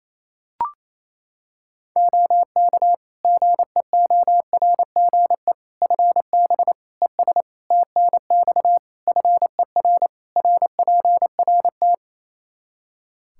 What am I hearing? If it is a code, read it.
Morse code, heard as a keyed tone, about 28 wpm, 700 Hz. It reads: OK GEORGE FB ES TNX FER RPRT
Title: K1USN SST QSO Practice 28wpm